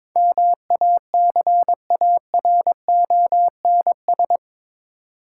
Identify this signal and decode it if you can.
Morse code, heard as a keyed tone, about 22 wpm, 700 Hz. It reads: MACARONS
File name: Difficult Words with Hints - 22wpm